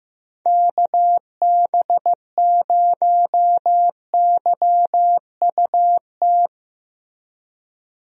Morse code, heard as a keyed tone, about 15 wpm, 700 Hz. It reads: KB0YUT
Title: USA Call Signs - 6 Characters - Repeated 2x - 15wpm